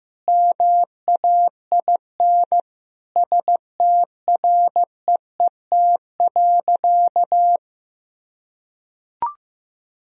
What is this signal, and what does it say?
Morse code, heard as a keyed tone, about 15 wpm, 700 Hz. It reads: MAIN STREET.
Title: Sentences from Top 1000 Words - Review Part 2 - 15wpm